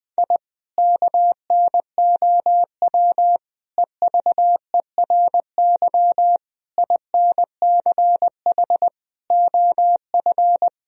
Morse code, heard as a keyed tone, about 20 wpm, 700 Hz. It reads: I KNOW EVERY INCH OF
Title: Sentences from Top 1000 Words - Review Part 3 - 20wpm